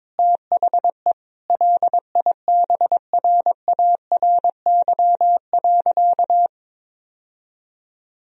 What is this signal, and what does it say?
Morse code, heard as a keyed tone, about 22 wpm, 700 Hz. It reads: THE LIBRARY.
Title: Sentences from Top 1000 Words - Review Part 4 - 22wpm